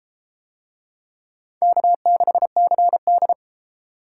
Morse code, heard as a keyed tone, about 32 wpm, 700 Hz. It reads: K6CD